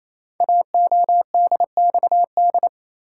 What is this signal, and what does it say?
Morse code, heard as a keyed tone, about 28 wpm, 700 Hz. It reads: AODXD
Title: Code Groups - 5 Letters - 28wpm